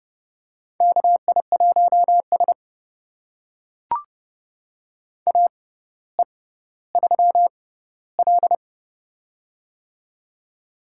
Morse code, heard as a keyed tone, about 31 wpm, spaced as for 16 wpm, 700 Hz. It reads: KI1S AE3L
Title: USA Call Signs - 4 Characters - 6x Character Spacing - 30wpm